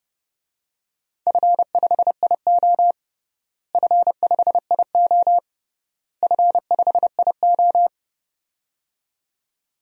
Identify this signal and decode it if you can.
Morse code, heard as a keyed tone, about 31 wpm, 700 Hz. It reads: F5IO F5IO F5IO
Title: DXCC-100 Call Signs - 4 Characters - Repeated 3x - 30wpm